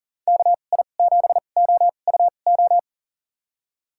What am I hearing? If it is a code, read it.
Morse code, heard as a keyed tone, about 37 wpm, spaced as for 28 wpm, 700 Hz. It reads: KI7OUO
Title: USA Call Signs - 2x Character Spacing - 40wpm